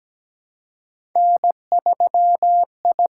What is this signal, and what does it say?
Morse code, heard as a keyed tone, about 17 wpm, 700 Hz. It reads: N3I
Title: USA Call Signs - 5 Characters - 6x Character Spacing - 17wpm